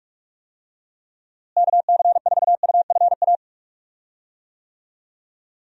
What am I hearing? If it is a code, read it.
Morse code, heard as a keyed tone, about 46 wpm, 700 Hz. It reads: KC4UFA
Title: USA Call Signs - 6 Characters - Repeated 2x - 45wpm